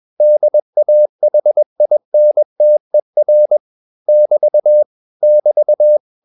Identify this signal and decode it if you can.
Morse code, heard as a keyed tone, about 21 wpm, 600 Hz. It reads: DAHINTER <BT> <BT>